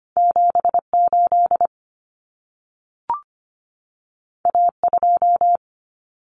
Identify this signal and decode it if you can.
Morse code, heard as a keyed tone, about 25 wpm, 700 Hz. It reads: 78 A2
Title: Common Cut Numbers - 2-Digit 25wpm